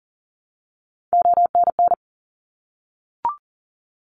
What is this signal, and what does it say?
Morse code, heard as a keyed tone, about 42 wpm, 700 Hz. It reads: ONN
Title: ARRL Sections 40wpm